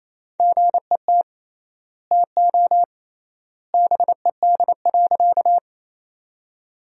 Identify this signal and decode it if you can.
Morse code, heard as a keyed tone, about 28 wpm, 700 Hz. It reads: GET TO BED.
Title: 3x Word-Spacing - Sentences from Top 500 Words - 28wpm